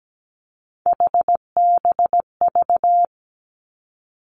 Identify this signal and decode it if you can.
Morse code, heard as a keyed tone, about 17 wpm, 700 Hz. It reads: HBV
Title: Three Challenging Letter-Number Combinations 17wpm